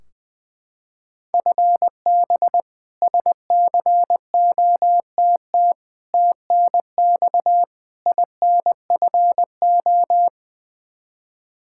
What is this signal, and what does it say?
Morse code, heard as a keyed tone, about 21 wpm, 700 Hz. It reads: FB SCOTT TNX INFO
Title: K1USN SST QSO Practice 20wpm